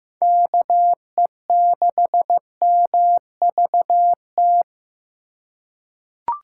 Morse code, heard as a keyed tone, about 15 wpm, 700 Hz. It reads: KE6MVT
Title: USA Call Signs - 6 Characters - 4x Character Spacing - 15wpm